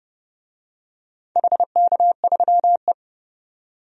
Morse code, heard as a keyed tone, about 30 wpm, 700 Hz. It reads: HK3E
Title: DXCC-100 Call Signs - 4 Characters - 30wpm